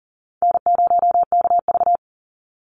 Morse code, heard as a keyed tone, about 40 wpm, 700 Hz. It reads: N0KV
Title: Easy Call Signs - 40wpm